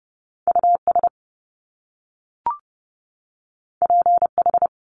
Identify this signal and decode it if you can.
Morse code, heard as a keyed tone, about 30 wpm, 700 Hz. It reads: US PH
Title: Top Two Letter Combinations in Words 30wpm